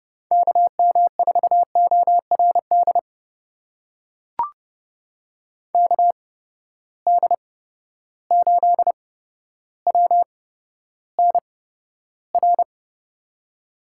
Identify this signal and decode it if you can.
Morse code, heard as a keyed tone, about 30 wpm, spaced as for 13 wpm, 700 Hz. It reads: KM4ORD KD8WNR
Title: USA Call Signs - 6 Characters - 8x Character Spacing - 30wpm